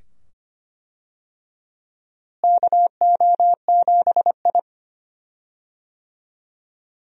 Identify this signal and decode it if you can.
Morse code, heard as a keyed tone, about 25 wpm, 700 Hz. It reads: KO7I
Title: USA Call Signs - 4 Characters - Repeated 2x - 25wpm